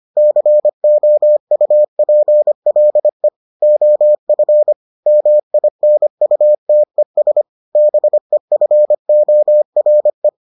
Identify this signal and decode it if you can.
Morse code, heard as a keyed tone, about 25 wpm, 600 Hz. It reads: COUPLE OF MINUTES BEFORE